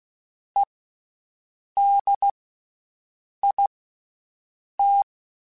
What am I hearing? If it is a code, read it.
Morse code, heard as a keyed tone, about 16 wpm, spaced as for 6 wpm, 800 Hz. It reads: EDIT
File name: Special INGs / Special INGs-mp3s / VerySlowToFaster Ing @15